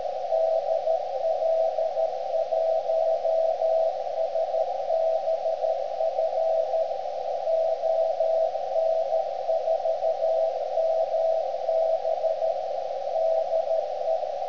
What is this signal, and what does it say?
Morse code, heard as a keyed tone, about 13 wpm, 650 Hz, with radio noise. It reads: DL1SBF ON4MB DE
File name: ft8-vs-cw / 90-to-150 CW 300hz